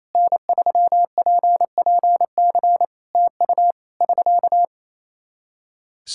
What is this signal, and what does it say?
Morse code, heard as a keyed tone, about 28 wpm, 700 Hz. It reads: N3PPC TU <SK>
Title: K1USN SST QSO Practice 28wpm